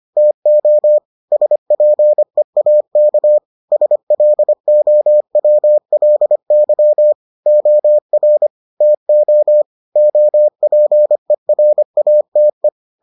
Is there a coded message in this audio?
Yes: Morse code from 0.2 to 12.7 s, heard as a keyed tone, about 25 wpm, 600 Hz. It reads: TO SPEAK SLOWLY OR TO OPERATE